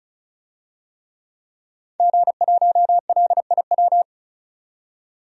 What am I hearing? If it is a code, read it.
Morse code, heard as a keyed tone, about 36 wpm, 700 Hz. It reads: G1LIW
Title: DXCC-100 Call Signs - 5 Characters - 35wpm